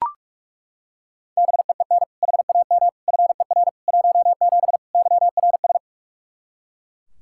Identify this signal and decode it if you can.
Morse code, heard as a keyed tone, about 42 wpm, 700 Hz. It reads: BEEN HAM FER 17 YRS
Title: Practice QSO - Exchange 3 - More Info - 45wpm